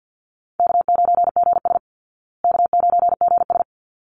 Morse code, heard as a keyed tone, about 48 wpm, 700 Hz. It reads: K9GS K9GS
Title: Call Signs Repeated 3x - 50wpm